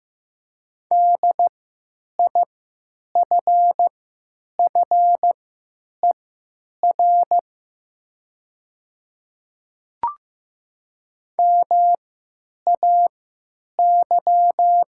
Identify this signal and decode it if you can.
Morse code, heard as a keyed tone, about 15 wpm, spaced as for 9 wpm, 700 Hz. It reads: DIFFER MAY
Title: 3x Character Spacing - Top 300 Words - 15wpm